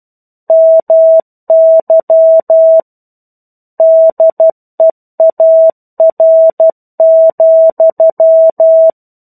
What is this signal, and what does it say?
Morse code, heard as a keyed tone, about 12 wpm, 650 Hz. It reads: MY DEAR,
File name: 01-Chapter-01 12 WPM